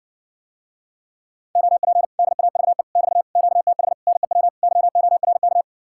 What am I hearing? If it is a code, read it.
Morse code, heard as a keyed tone, about 60 wpm, 700 Hz. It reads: OP DAVE <BT> QTH NEW YORK